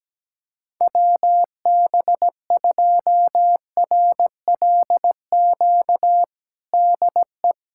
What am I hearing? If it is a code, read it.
Morse code, heard as a keyed tone, about 17 wpm, 700 Hz. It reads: WB2RLQ DE
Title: USA Callsign DE Callsign - 6 Characters - 17wpm